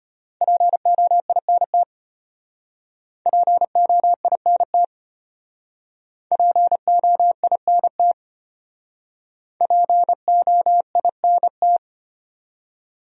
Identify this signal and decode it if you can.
Morse code, heard as a keyed tone, about 32 wpm, 700 Hz. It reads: POINT POINT POINT POINT